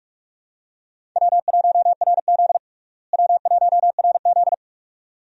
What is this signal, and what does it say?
Morse code, heard as a keyed tone, about 42 wpm, 700 Hz. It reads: W1RZ W1RZ
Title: USA Call Signs - 4 Characters - Repeated 2x - 45wpm